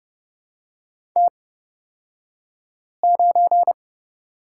Morse code, heard as a keyed tone, about 30 wpm, spaced as for 5 wpm, 700 Hz. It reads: T9
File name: Lesson 38d - Letter q - Two Characters 30wpm